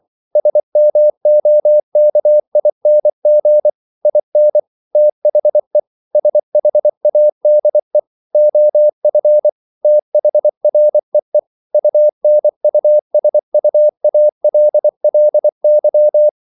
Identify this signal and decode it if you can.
Morse code, heard as a keyed tone, about 24 wpm, 600 Hz. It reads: SMOKING IN THE SHADE OF THREE UNUSUALLY